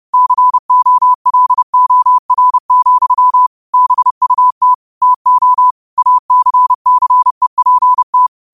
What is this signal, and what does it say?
Morse code, heard as a keyed tone, about 30 wpm, 1000 Hz. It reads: COLOR, BUT TO ACCEPT
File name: qotdfiles / 30wpm